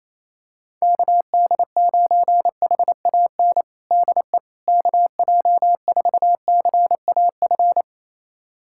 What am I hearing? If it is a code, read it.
Morse code, heard as a keyed tone, about 28 wpm, 700 Hz. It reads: KD9HAN DE KJ4CAF
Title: USA Callsign DE Callsign 28wpm